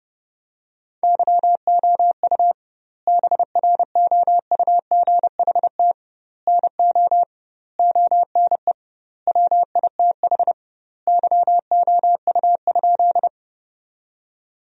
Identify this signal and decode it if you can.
Morse code, heard as a keyed tone, about 30 wpm, 700 Hz. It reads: YOU BROUGHT NO ONE WITH YOU?
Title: 2x Word-Spacing - Sentences from Top 500 Words - 30wpm